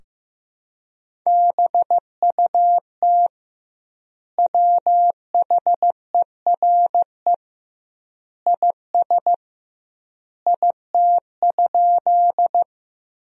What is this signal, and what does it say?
Morse code, heard as a keyed tone, about 15 wpm, 700 Hz. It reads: BUT WHERE IS IT?
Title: 2x Word-Spacing - Sentences from Top 200 Words - 15wpm